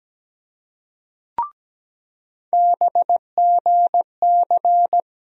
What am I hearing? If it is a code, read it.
Morse code, heard as a keyed tone, about 17 wpm, 700 Hz. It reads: BGC
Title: USA Call Sign Postfixes - 3 Letters 17wpm